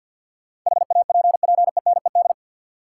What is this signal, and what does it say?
Morse code, heard as a keyed tone, about 53 wpm, 700 Hz. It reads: HAPPENED
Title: Sentences from Top 1000 Words - Review Part 2 - New Word Drill - 50wpm